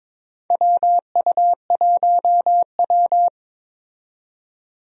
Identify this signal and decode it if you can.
Morse code, heard as a keyed tone, about 22 wpm, 700 Hz. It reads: WU1W
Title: USA Call Signs - 4 Characters - 22wpm